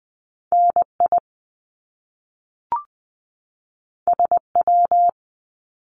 Morse code, heard as a keyed tone, about 20 wpm, 700 Hz. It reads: N I S W
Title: Top Two Letter Combinations in Words 20wpm